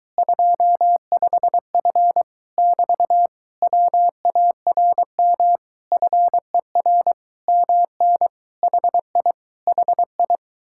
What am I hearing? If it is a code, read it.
Morse code, heard as a keyed tone, about 24 wpm, 700 Hz. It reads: 25F <BT> WARM FER MN HI HI